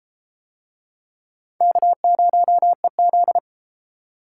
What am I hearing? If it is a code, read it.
Morse code, heard as a keyed tone, about 33 wpm, 700 Hz. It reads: K0EZ